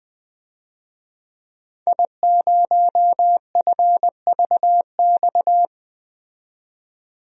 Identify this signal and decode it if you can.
Morse code, heard as a keyed tone, about 20 wpm, 700 Hz. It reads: I0FVX
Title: DXCC-100 Call Signs - 5 Characters - 20wpm